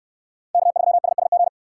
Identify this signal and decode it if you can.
Morse code, heard as a keyed tone, about 69 wpm, 700 Hz. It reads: K3SSG